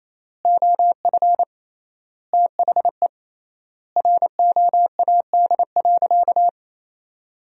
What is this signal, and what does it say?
Morse code, heard as a keyed tone, about 28 wpm, 700 Hz. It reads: OF THE ROAD.
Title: 3x Word-Spacing - Sentences from Top 400 Words - 28wpm